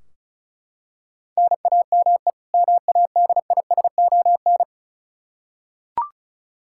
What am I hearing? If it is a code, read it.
Morse code, heard as a keyed tone, about 34 wpm, 700 Hz. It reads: NAME MADISON
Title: Practice QSO - Exchange 1 - Beginning - 35wpm